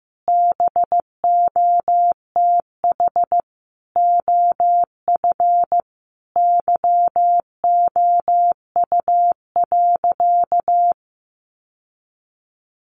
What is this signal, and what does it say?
Morse code, heard as a keyed tone, about 15 wpm, 700 Hz. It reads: BOTH OF YOU.
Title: Sentences from Top 1200 Words 15wpm